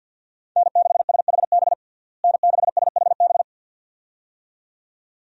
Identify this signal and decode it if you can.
Morse code, heard as a keyed tone, about 47 wpm, 700 Hz. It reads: N6SHB N6SHB
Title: USA Call Signs - 5 Characters - Repeated 2x - 50wpm